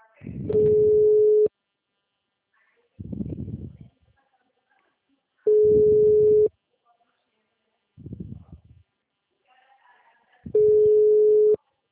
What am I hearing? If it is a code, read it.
Ringback tone (Europe, 425 Hz).
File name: out- -2024- - - .23033